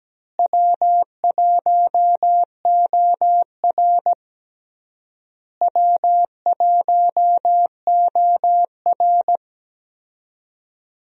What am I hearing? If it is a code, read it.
Morse code, heard as a keyed tone, about 17 wpm, 700 Hz. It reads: W1OR W1OR
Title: USA Call Signs - 4 Characters - Repeated 3x - 17wpm